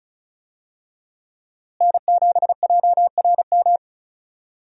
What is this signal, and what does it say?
Morse code, heard as a keyed tone, about 36 wpm, 700 Hz. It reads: N7JRM
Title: USA Call Signs - 5 Characters - Repeated 2x - 35wpm